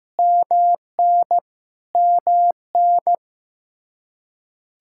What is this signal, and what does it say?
Morse code, heard as a keyed tone, about 15 wpm, 700 Hz. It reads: MN MN
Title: Practice QSO - Exchange 1 - Beginning - 15wpm